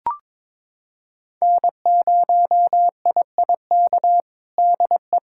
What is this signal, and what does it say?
Morse code, heard as a keyed tone, about 22 wpm, 700 Hz. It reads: N0IIK DE
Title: USA Callsign DE Callsign - 5 Characters - 22wpm